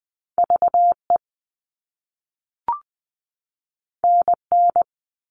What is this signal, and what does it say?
Morse code, heard as a keyed tone, about 20 wpm, 700 Hz. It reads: VE NN
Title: All Cut Numbers - 2-Digit 20wpm